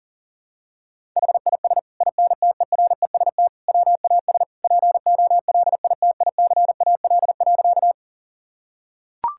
Morse code, heard as a keyed tone, about 37 wpm, 690 Hz. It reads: HIS INTEREST WAS POLITICAL.
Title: Sentences from Top 1000 Words - Review Part 5 - 40wpm